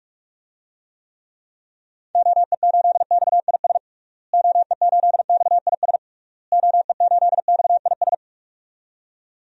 Morse code, heard as a keyed tone, about 46 wpm, 700 Hz. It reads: OE8XIS OE8XIS OE8XIS
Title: DXCC-100 Call Signs - 6 Characters - Repeated 3x - 45wpm